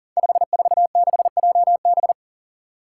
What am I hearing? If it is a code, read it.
Morse code, heard as a keyed tone, about 40 wpm, 700 Hz. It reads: 546JB
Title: Five Challenging Letter-Number Combinations 40wpm